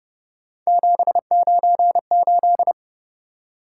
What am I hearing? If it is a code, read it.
Morse code, heard as a keyed tone, about 29 wpm, 700 Hz. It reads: 798
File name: Speed-Racing - Three Letter-Number Combinations - 25wpm